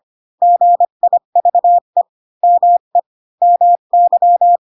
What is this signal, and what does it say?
Morse code, heard as a keyed tone, about 25 wpm, 700 Hz. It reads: GIVE ME MY
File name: Pinocchio / Pinocchio ch3 20 25 ik1hge